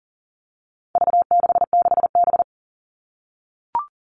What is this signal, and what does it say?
Morse code, heard as a keyed tone, about 40 wpm, 700 Hz. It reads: V66B
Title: All Cut Numbers - 4-Digit 40wpm